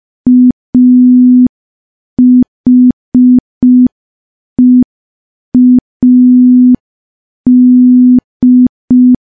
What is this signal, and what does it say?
Morse code, heard as a keyed tone, about 5 wpm, 250 Hz. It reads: AHEAD
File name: morse / morse-training-05WPM